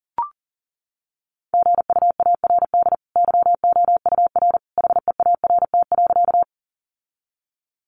Morse code, heard as a keyed tone, about 40 wpm, 700 Hz. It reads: GUARD YOUR HEART.
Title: Sentences from Top 1200 Words 40wpm